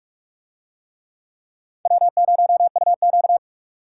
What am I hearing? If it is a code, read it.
Morse code, heard as a keyed tone, about 39 wpm, 700 Hz. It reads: W0UQ